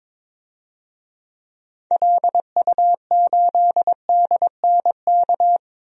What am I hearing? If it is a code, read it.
Morse code, heard as a keyed tone, about 22 wpm, 700 Hz. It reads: LU8DNK
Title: DXCC-100 Call Signs - 6 Characters - 22wpm